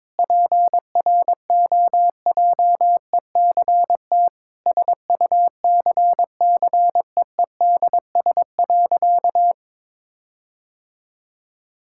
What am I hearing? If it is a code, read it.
Morse code, heard as a keyed tone, about 21 wpm, 700 Hz. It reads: PROJECT SUCCEEDS.
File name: practice / 22wpm / Sentences with 8-Word Phrases - 22wpm